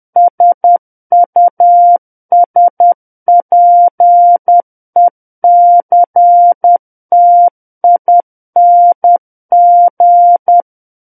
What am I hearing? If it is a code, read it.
Morse code, heard as a keyed tone, about 10 wpm, 700 Hz. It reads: SUSPECTING
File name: Book One / 01-Chapter 1 - 10 WPM